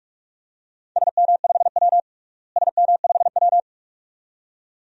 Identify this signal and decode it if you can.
Morse code, heard as a keyed tone, about 46 wpm, 700 Hz. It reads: SM5W SM5W
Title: DXCC-100 Call Signs - 4 Characters - Repeated 2x - 45wpm